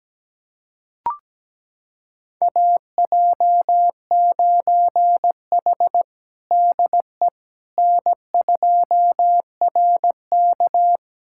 Morse code, heard as a keyed tone, about 17 wpm, 700 Hz. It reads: AJ9H DE N2RK
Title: USA Callsign DE Callsign - 4 Characters - 17wpm